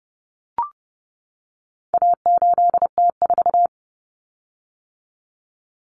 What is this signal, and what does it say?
Morse code, heard as a keyed tone, about 30 wpm, 700 Hz. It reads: A8T4
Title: Common Cut Numbers - 4-Digit 30wpm